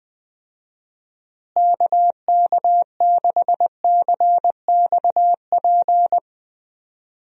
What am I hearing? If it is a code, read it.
Morse code, heard as a keyed tone, about 20 wpm, 700 Hz. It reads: KK6CXP
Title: USA Call Signs - 6 Characters - Repeated 2x - 20wpm